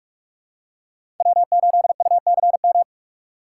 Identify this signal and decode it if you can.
Morse code, heard as a keyed tone, about 47 wpm, 700 Hz. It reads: W8UCM
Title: USA Call Signs - 5 Characters - 6x Character Spacing - 45wpm